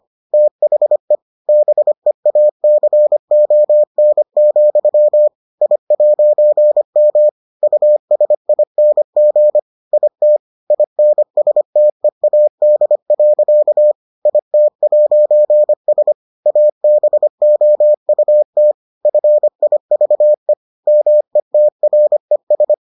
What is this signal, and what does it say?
Morse code, heard as a keyed tone, about 25 wpm, 600 Hz. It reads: THE BEACON, I'M USING IT INSTEAD. IT'S ABOUT FIVE METRES